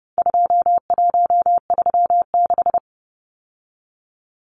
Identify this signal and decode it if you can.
Morse code, heard as a keyed tone, about 30 wpm, 700 Hz. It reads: 2136
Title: Numbers - 4-Digit 30wpm